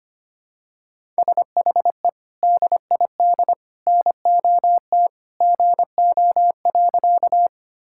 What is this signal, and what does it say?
Morse code, heard as a keyed tone, about 25 wpm, 700 Hz. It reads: SHE DID NOT GO.
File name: practice / 25wpm / Sentences with 3-Word Phrases - 25wpm